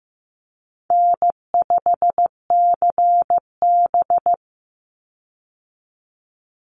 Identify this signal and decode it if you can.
Morse code, heard as a keyed tone, about 15 wpm, 700 Hz. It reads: N5CB
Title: USA Call Signs 15wpm